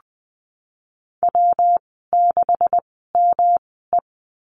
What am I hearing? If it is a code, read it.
Morse code, heard as a keyed tone, about 20 wpm, spaced as for 15 wpm, 700 Hz. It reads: W6ME
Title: 140 Lesson 23g - Letter f - Call Signs 20wpm